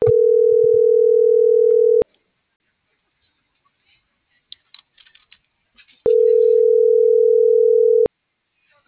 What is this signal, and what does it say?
Ringback tone (North America), 440 + 480 Hz.